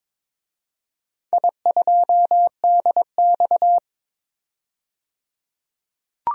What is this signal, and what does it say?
Morse code, heard as a keyed tone, about 21 wpm, 700 Hz. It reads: I2DX
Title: DXCC-100 Call Signs - 4 Characters - 22wpm